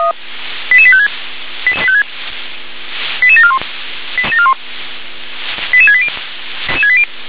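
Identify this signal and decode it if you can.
Five-tone selective call (ZVEI), heard as quick runs of five tones, tones of about 70 ms. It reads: 88856 88741 88680
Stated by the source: ZVEI five-tone selective call received on SDR.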